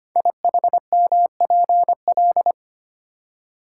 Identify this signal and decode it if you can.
Morse code, heard as a keyed tone, about 25 wpm, 700 Hz. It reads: IHMPL
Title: Code Groups - 5 Letters - 25wpm